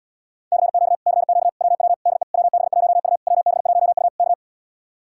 Morse code, heard as a keyed tone, about 80 wpm, 700 Hz. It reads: CQ CQ WW DE WB1U WB1U K